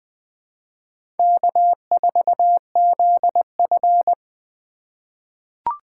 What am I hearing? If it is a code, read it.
Morse code, heard as a keyed tone, about 20 wpm, 700 Hz. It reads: K4ZF
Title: USA Call Signs - 4 Characters - 4x Character Spacing - 20wpm